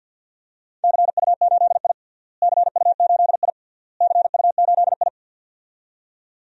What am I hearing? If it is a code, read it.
Morse code, heard as a keyed tone, about 48 wpm, 700 Hz. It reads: CU8I CU8I CU8I
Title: DXCC-100 Call Signs - 4 Characters - Repeated 3x - 50wpm